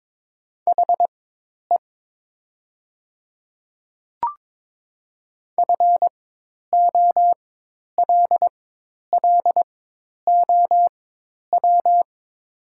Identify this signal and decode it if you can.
Morse code, heard as a keyed tone, about 22 wpm, spaced as for 10 wpm, 700 Hz. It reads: HE FOLLOW
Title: 4x Character Spacing - Top 200 Words - 22wpm